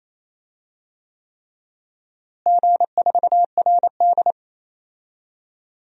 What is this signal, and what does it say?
Morse code, heard as a keyed tone, about 28 wpm, 700 Hz. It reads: G4RD
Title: DXCC-100 Call Signs - 4 Characters - Repeated 3x - 28wpm